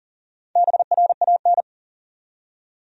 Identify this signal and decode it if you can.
Morse code, heard as a keyed tone, about 40 wpm, 700 Hz. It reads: BRAN
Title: USA Common Names - Max 5 Letters - 40wpm